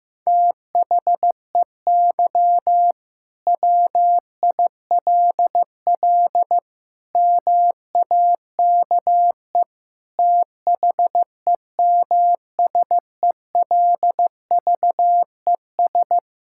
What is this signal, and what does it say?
Morse code, heard as a keyed tone, about 15 wpm, 700 Hz. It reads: THEY WILL MAKE THEMSELVES
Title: Sentences from Top 1600 Words 15wpm